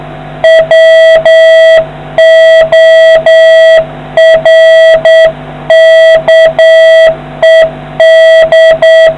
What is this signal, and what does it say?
Morse code, heard as a keyed tone, about 8 wpm, 650 Hz, with noise. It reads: WORKED